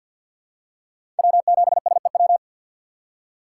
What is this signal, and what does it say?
Morse code, heard as a keyed tone, about 51 wpm, 690 Hz. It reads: W7SEW
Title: USA Call Signs - 5 Characters - 3x Character Spacing - 50wpm